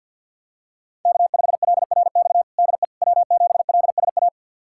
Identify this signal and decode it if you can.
Morse code, heard as a keyed tone, about 53 wpm, 700 Hz. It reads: K5LRX DE W7LSA
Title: USA Callsign DE Callsign 50wpm